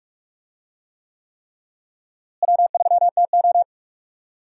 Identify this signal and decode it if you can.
Morse code, heard as a keyed tone, about 45 wpm, 700 Hz. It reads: W3TO
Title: USA Call Signs - 4 Characters - Repeated 3x - 45wpm